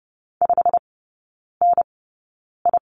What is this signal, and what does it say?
Morse code, heard as a keyed tone, about 30 wpm, spaced as for 9 wpm, 700 Hz. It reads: 5NI
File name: course / 30:10wpm / 087 Lesson 15e - Number 5 - Three Characters 30wpm